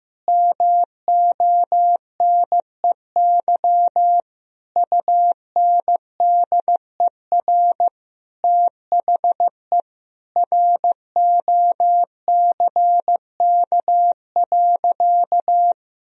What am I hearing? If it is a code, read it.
Morse code, heard as a keyed tone, about 15 wpm, 700 Hz. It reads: MONEY UNDER THE ROCK.
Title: Sentences from Top 1000 Words - Review Part 2 - 15wpm